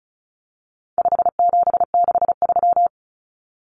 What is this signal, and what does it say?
Morse code, heard as a keyed tone, about 34 wpm, 700 Hz. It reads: 5763
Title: Numbers - 4-Digit 35wpm